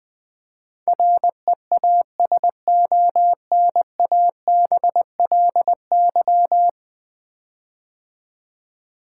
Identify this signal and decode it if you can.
Morse code, heard as a keyed tone, about 20 wpm, 700 Hz. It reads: REASONABLY
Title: 10 Letter Words - First 3 Letters Given - 20wpm